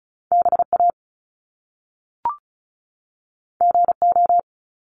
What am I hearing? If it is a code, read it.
Morse code, heard as a keyed tone, about 36 wpm, 700 Hz. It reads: BA GO
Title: Top Two Letter Combinations in Words 35wpm